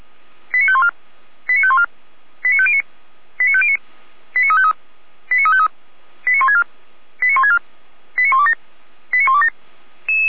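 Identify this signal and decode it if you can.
Five-tone selective call (ZVEI), heard as quick runs of five tones, tones of about 70 ms. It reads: 89514 89514 89509 89509 89353 89353 89164 89164 89127 89127
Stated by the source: ZVEI five-tone selective call received on SDR.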